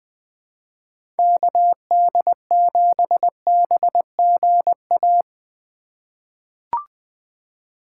Morse code, heard as a keyed tone, about 21 wpm, 700 Hz. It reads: KD7BGA
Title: USA Call Signs - 6 Characters - 2x Character Spacing - 20wpm